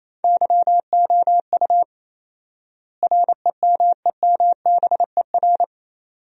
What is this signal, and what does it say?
Morse code, heard as a keyed tone, about 28 wpm, 700 Hz. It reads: YOU REMEMBER
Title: 4x Word-Spacing - Sentences from Top 400 Words - 28wpm